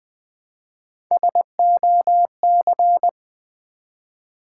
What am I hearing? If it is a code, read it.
Morse code, heard as a keyed tone, about 20 wpm, 700 Hz. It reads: SOC
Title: Build A Word - 1 Letter At A Time - 4-Letter Words - 20wpm